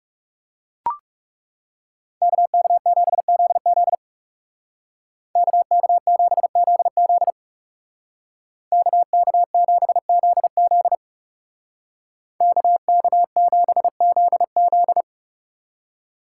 Morse code, heard as a keyed tone, about 39 wpm, 700 Hz. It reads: KK7ZZ KK7ZZ KK7ZZ KK7ZZ